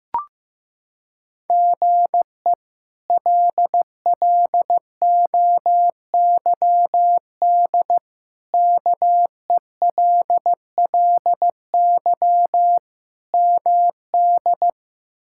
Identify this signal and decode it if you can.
Morse code, heard as a keyed tone, about 15 wpm, 700 Hz. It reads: GE LLOYD KELLY MD
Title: K1USN SST QSO Practice - 15wpm